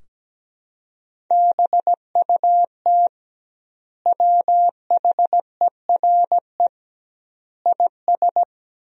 Morse code, heard as a keyed tone, about 17 wpm, 700 Hz. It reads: BUT WHERE IS
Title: 2x Word-Spacing - Sentences from Top 200 Words - 17wpm